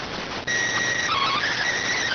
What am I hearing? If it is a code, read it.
SSTV (slow-scan television), PD 290.